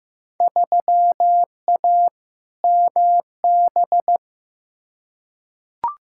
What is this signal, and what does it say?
Morse code, heard as a keyed tone, about 15 wpm, 700 Hz. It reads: 3A MB
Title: Field Day Exchanges Encore - 15wpm